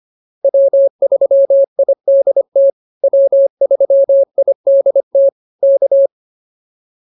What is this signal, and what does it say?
Morse code, heard as a keyed tone, about 25 wpm, 590 Hz. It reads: W3IDT W3IDT K